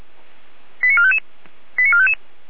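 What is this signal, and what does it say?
Five-tone selective call (ZVEI), heard as quick runs of five tones, tones of about 70 ms. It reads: 89455 89455
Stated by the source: ZVEI five-tone selective call 88074 received by SDR, logged by feuerwehr-orlamuende fire brigade server.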